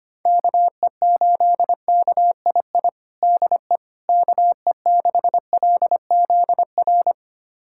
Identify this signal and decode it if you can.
Morse code, heard as a keyed tone, about 25 wpm, 700 Hz. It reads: KE8KII DE KE6LZR
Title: USA Callsign DE Callsign - 6 Characters - 25wpm